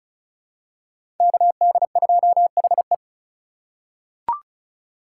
Morse code, heard as a keyed tone, about 36 wpm, 700 Hz. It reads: KD2HE
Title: K1USN SST QSO Practice - 35wpm